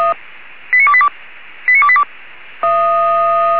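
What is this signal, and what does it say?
Five-tone selective call (ZVEI), heard as quick runs of five tones, tones of about 70 ms. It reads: 89282 89282
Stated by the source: ZVEI five-tone selective call received on an SDR.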